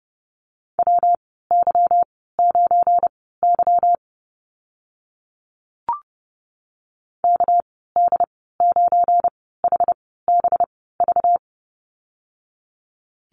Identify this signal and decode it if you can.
Morse code, heard as a keyed tone, about 30 wpm, spaced as for 17 wpm, 700 Hz. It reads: WY9Y KD9HBV
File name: Lesson 37g - Letter x - Call Signs 30wpm